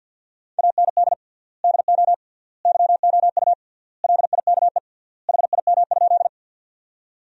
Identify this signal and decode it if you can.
Morse code, heard as a keyed tone, about 52 wpm, 700 Hz. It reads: AND DO YOU LIKE HIM?
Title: 3x Word-Spacing - Sentences from Top 100 Words - 50wpm